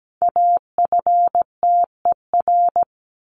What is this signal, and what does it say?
Morse code, heard as a keyed tone, about 17 wpm, 700 Hz. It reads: AFTER
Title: Top 300 Words Repeated 17wpm